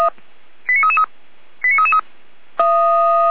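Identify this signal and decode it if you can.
Five-tone selective call (ZVEI), heard as quick runs of five tones, tones of about 70 ms. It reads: 89333 89333
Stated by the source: ZVEI five-tone selective call 89230 received by SDR.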